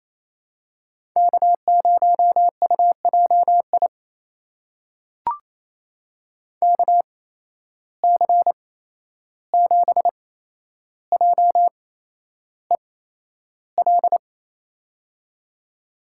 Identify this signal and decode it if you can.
Morse code, heard as a keyed tone, about 28 wpm, spaced as for 11 wpm, 700 Hz. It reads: K0UJI KC7JEL
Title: USA Call Signs - 8x Character Spacing - 28wpm